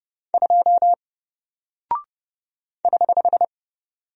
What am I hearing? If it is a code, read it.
Morse code, heard as a keyed tone, about 30 wpm, 700 Hz. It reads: 2 <HH>
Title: Single Character (Letter:Number:Punctuation:Prosign) 30wpm